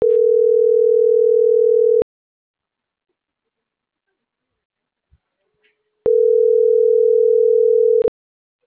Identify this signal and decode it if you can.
Ringback tone (North America), 440 + 480 Hz.